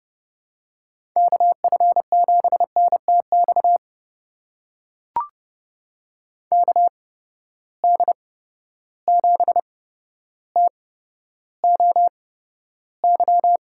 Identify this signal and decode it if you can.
Morse code, heard as a keyed tone, about 32 wpm, spaced as for 13 wpm, 700 Hz. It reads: KF7NTX KD7TOY
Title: USA Call Signs - 6 Characters - 8x Character Spacing - 30wpm